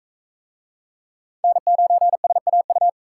Morse code, heard as a keyed tone, about 44 wpm, 700 Hz. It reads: N9SAU